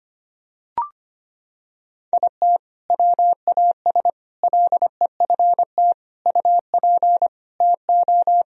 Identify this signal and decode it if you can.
Morse code, heard as a keyed tone, about 25 wpm, 700 Hz. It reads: IT WAS LEFT UP TO
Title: Sentences from Top 1000 Words - Review Part 4 - 25wpm